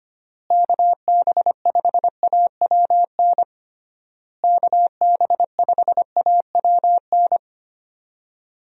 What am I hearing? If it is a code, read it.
Morse code, heard as a keyed tone, about 25 wpm, 700 Hz. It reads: KB5AWN KB5AWN
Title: USA Call Signs - 6 Characters - Repeated 3x - 25wpm